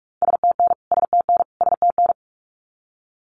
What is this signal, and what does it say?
Morse code, heard as a keyed tone, about 44 wpm, 700 Hz. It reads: STN STN STN
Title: Abbreviations Repeated 3x - 45wpm